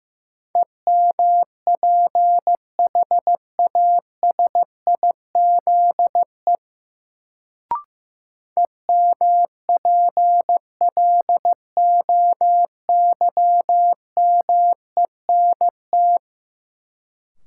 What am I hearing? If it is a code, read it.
Morse code, heard as a keyed tone, about 15 wpm, 700 Hz. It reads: EMPHASIZE EMPLOYMENT
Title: Sentences from Top 1500 Words - New Word Drill - 15wpm